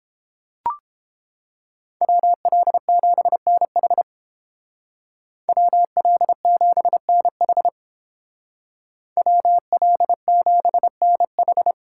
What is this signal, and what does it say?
Morse code, heard as a keyed tone, about 29 wpm, 700 Hz. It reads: WL7NH WL7NH WL7NH